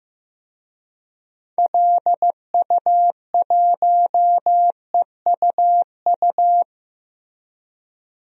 Morse code, heard as a keyed tone, about 15 wpm, 700 Hz. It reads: LU1EUU
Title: DXCC-100 Call Signs - 6 Characters - Repeated 2x - 15wpm